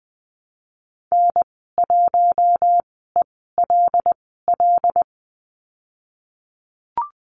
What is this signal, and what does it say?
Morse code, heard as a keyed tone, about 20 wpm, spaced as for 15 wpm, 700 Hz. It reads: N1ELL
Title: 128 Lesson 21g - Number 6 - Call Signs 20wpm